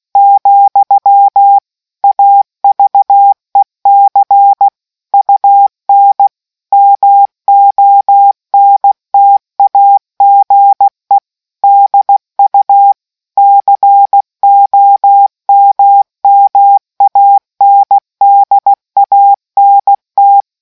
Morse code, heard as a keyed tone, about 16 wpm, 800 Hz. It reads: , AVEC UN MONTAGE DU COMMANDANT